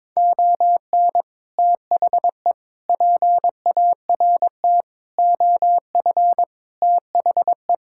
Morse code, heard as a keyed tone, about 22 wpm, 700 Hz. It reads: ON THE PART OF THE
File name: practice / 22wpm / Build A Phrase - 5-Word Phrases - 22wpm